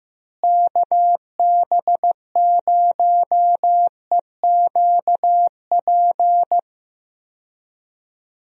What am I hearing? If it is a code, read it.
Morse code, heard as a keyed tone, about 15 wpm, 700 Hz. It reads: KB0EQP
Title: USA Call Signs - 6 Characters - Repeated 2x - 15wpm